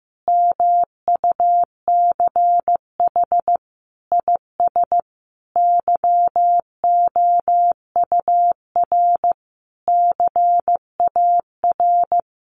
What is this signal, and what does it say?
Morse code, heard as a keyed tone, about 15 wpm, 700 Hz. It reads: MUCH IS YOUR CAR
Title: Sentences from Top 1100 Words 15wpm